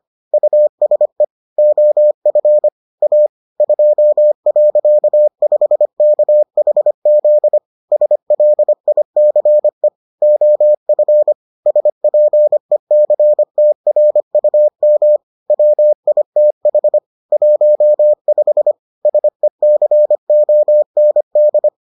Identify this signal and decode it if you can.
Morse code, heard as a keyed tone, about 25 wpm, 600 Hz. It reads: USE OF A 2.5KHZ SLICE OF SPECTRUM WITH 15 SECOND